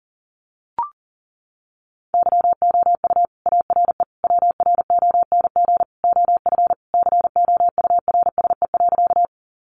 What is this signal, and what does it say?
Morse code, heard as a keyed tone, about 40 wpm, 700 Hz. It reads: YOU ARE WRONG OF COURSE.
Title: Sentences from Top 900 Words 40wpm